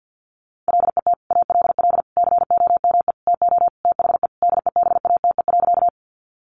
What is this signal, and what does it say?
Morse code, heard as a keyed tone, about 50 wpm, 700 Hz. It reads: LET ALL COME TO THE DEBATE.
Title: Sentences from Top 1100 Words 50wpm